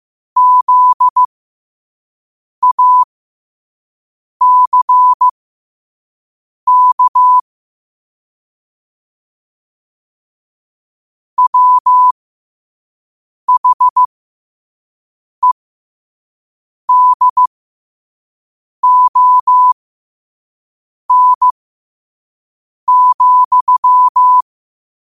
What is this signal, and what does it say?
Morse code, heard as a keyed tone, about 15 wpm, spaced as for 5 wpm, 1000 Hz. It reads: ZACK WHEDON,
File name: qotdfiles / 5wpm